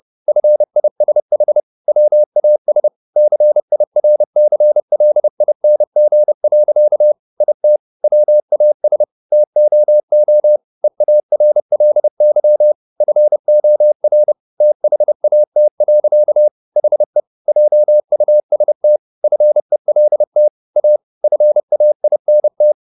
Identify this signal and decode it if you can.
Morse code, heard as a keyed tone, about 30 wpm, 600 Hz. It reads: FISH WAS CIRCLING. IT WAS TOO EARLY FOR THAT. HE JUST FELT A FAINT